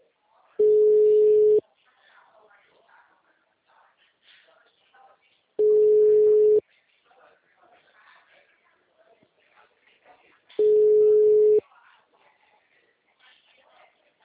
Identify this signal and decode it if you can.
Ringback tone (Europe, 425 Hz), with speech.
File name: out- -2013- - - .85